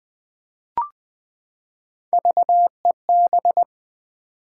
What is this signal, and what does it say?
Morse code, heard as a keyed tone, about 20 wpm, 700 Hz. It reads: VEB
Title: Three Letter-Number Combinations - 20wpm